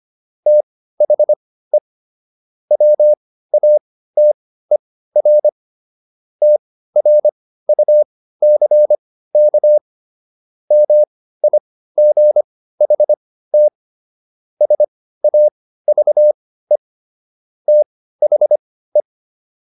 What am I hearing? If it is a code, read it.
Morse code, heard as a keyed tone, about 25 wpm, spaced as for 15 wpm, 600 Hz. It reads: THE WATER TRUCK MIGHT SAVE THE